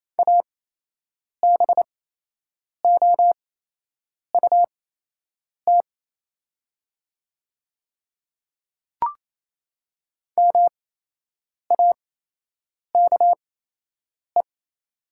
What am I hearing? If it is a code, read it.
Morse code, heard as a keyed tone, about 28 wpm, spaced as for 8 wpm, 700 Hz. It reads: ABOUT MAKE
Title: 8x Character Spacing - Top 300 Words - 28wpm